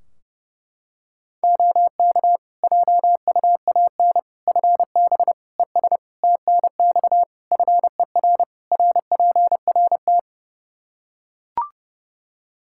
Morse code, heard as a keyed tone, about 30 wpm, 700 Hz. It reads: OK JUAN FB ES TNX FER RPRT
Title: K1USN SST QSO Practice 30wpm